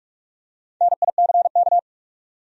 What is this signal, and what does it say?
Morse code, heard as a keyed tone, about 46 wpm, 700 Hz. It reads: NICK
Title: USA Common Names - Max 4 Letters - 45wpm